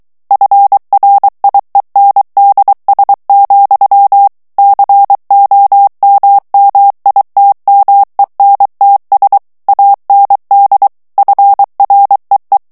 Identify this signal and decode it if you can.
Morse code, heard as a keyed tone, about 23 wpm, 800 Hz. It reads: FRIENDS, COMMITMENTS AND FREE
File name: morse mp3 / ANGLAIS texte clair 1 vitesse 22 mots minute